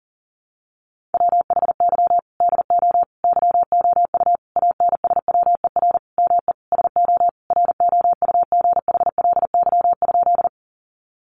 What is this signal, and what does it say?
Morse code, heard as a keyed tone, about 40 wpm, 700 Hz. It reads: WHY DO YOU ANSWER ME SO ROUGHLY?
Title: Sentences from Top 1500 Words 40wpm